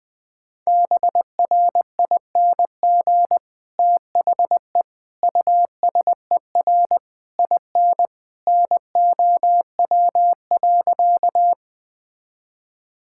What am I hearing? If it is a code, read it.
Morse code, heard as a keyed tone, about 21 wpm, 700 Hz. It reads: BRING THE USER IN NOW.
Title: Sentences from Top 1000 Words - Review Part 5 - 20wpm